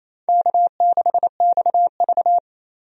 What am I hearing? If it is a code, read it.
Morse code, heard as a keyed tone, about 28 wpm, 700 Hz. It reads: K6XV
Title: USA Call Signs - 4 Characters - 28wpm